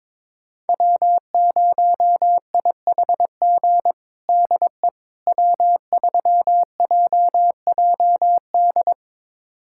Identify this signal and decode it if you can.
Morse code, heard as a keyed tone, about 22 wpm, 700 Hz. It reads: W0IHG DE W3JJD
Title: USA Callsign DE Callsign - 5 Characters - 22wpm